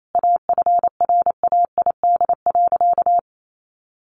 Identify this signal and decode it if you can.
Morse code, heard as a keyed tone, about 28 wpm, 700 Hz. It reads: AFRAID.
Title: Sentences from Top 1500 Words 28wpm